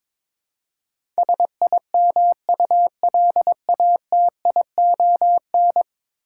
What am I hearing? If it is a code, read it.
Morse code, heard as a keyed tone, about 23 wpm, 700 Hz. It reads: SIMULATION
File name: practice / 22wpm / 10 Letter Words - First 3 Letters Given - 22wpm